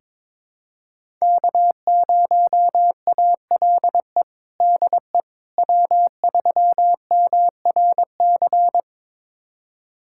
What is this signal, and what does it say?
Morse code, heard as a keyed tone, about 22 wpm, 700 Hz. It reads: K0ALE DE W3MRC
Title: USA Callsign DE Callsign - 5 Characters - 22wpm